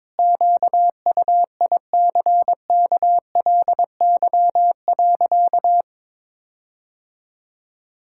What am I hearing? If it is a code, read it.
Morse code, heard as a keyed tone, about 22 wpm, 700 Hz. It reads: QUICKLY.
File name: Sentences with 7-Word Phrases - 22wpm